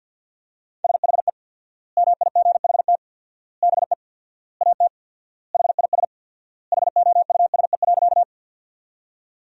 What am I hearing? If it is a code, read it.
Morse code, heard as a keyed tone, about 48 wpm, 700 Hz. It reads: SHE MIGHT BE AT HIS HOUSE.
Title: 4x Word-Spacing - Sentences from Top 300 Words - 50wpm